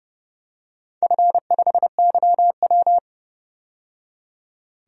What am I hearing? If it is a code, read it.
Morse code, heard as a keyed tone, about 31 wpm, 700 Hz. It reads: F5YW
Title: DXCC-100 Call Signs - 4 Characters - 30wpm